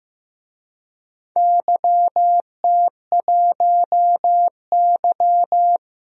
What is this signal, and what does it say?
Morse code, heard as a keyed tone, about 16 wpm, 700 Hz. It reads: YT1Y
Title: DXCC-100 Call Signs - 4 Characters - Repeated 2x - 15wpm